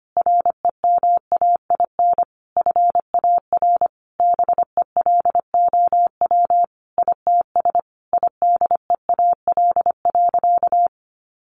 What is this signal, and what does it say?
Morse code, heard as a keyed tone, about 25 wpm, 700 Hz. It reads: REMAIN FAR BELOW ITS IDEAL.
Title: Sentences from Top 1400 Words 25wpm